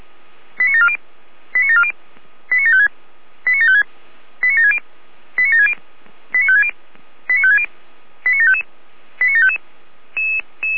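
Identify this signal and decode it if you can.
Five-tone selective call (ZVEI), heard as quick runs of five tones, tones of about 70 ms. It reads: 89740 89740 89756 89756 89760 89760 89560 89560 89755 89755
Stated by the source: ZVEI five-tone call 89169 received by SDR.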